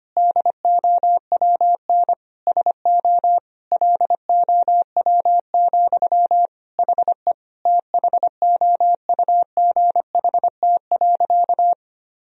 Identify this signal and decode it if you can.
Morse code, heard as a keyed tone, about 25 wpm, 700 Hz. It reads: DOWN SO LOW, HE THOUGHT.
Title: Sentences from Top 1000 Words - Review Part 1 - 25wpm